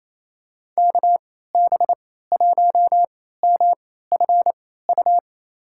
Morse code, heard as a keyed tone, about 27 wpm, spaced as for 16 wpm, 700 Hz. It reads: KB1MFU
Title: USA Call Signs - 3x Character Spacing - 28wpm